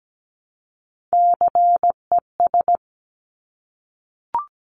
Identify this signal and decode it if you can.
Morse code, heard as a keyed tone, about 17 wpm, 700 Hz. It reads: CES
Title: Top Three Letter Combinations in Words 17wpm